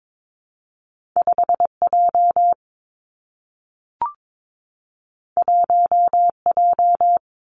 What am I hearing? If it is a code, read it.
Morse code, heard as a keyed tone, about 22 wpm, 700 Hz. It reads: 5 J 1 J
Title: Two Challenging Letter-Number Combinations 22wpm